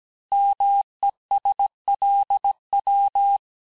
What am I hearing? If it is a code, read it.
Morse code, heard as a keyed tone, about 17 wpm, 800 Hz. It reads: MESLW
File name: cours-du-lundi files / cours-1587-texte-chiffre-vitesse-16MPM